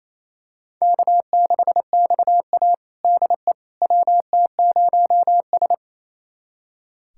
Morse code, heard as a keyed tone, about 28 wpm, 700 Hz. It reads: K6XA DE WT0S
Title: USA Callsign DE Callsign - 4 Characters - 28wpm